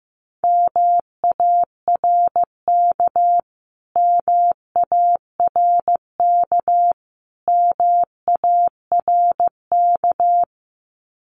Morse code, heard as a keyed tone, about 15 wpm, 700 Hz. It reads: MARK MARK MARK
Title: Top 500 Words Repeated 15wpm